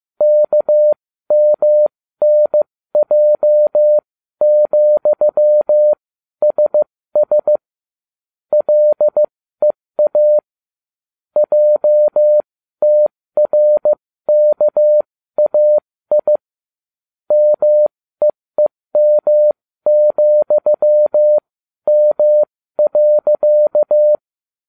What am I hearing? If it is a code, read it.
Morse code, heard as a keyed tone, about 15 wpm, spaced as for 13 wpm, 600 Hz. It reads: KMNJ,SS LEA JTRKAI MEEM,M.